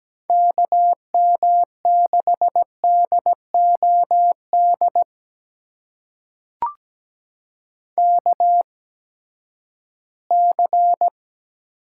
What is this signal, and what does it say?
Morse code, heard as a keyed tone, about 17 wpm, spaced as for 12 wpm, 700 Hz. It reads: KM6DOD KC
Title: USA Call Signs - 6 Characters - 8x Character Spacing - 17wpm